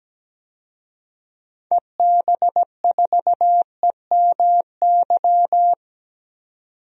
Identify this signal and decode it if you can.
Morse code, heard as a keyed tone, about 17 wpm, 700 Hz. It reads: EB4EMY
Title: DXCC-100 Call Signs - 6 Characters - 17wpm